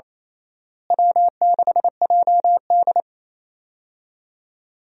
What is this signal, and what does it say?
Morse code, heard as a keyed tone, about 28 wpm, 700 Hz. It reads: W6JD
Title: USA Call Signs - 4 Characters - Repeated 3x - 28wpm